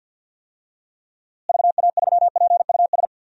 Morse code, heard as a keyed tone, about 53 wpm, 700 Hz. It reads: VA3PUS
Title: DXCC-100 Call Signs - 6 Characters - Repeated 2x - 50wpm